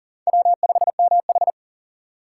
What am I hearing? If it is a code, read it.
Morse code, heard as a keyed tone, about 40 wpm, 700 Hz. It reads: W5MH
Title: USA Call Signs - 4 Characters - 2x Character Spacing - 40wpm